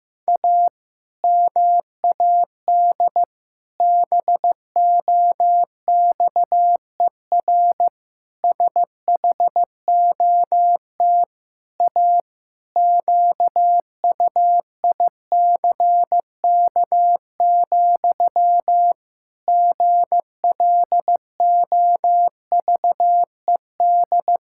Morse code, heard as a keyed tone, about 15 wpm, 700 Hz. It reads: A MAD BOXER SHOT A QUICK, GLOVED
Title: Pangram Challenge v2 - 15wpm